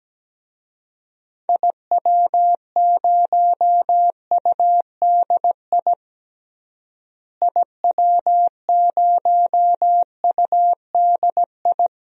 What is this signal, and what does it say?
Morse code, heard as a keyed tone, about 17 wpm, 700 Hz. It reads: IW0UDI IW0UDI
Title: DXCC-100 Call Signs - 6 Characters - Repeated 2x - 17wpm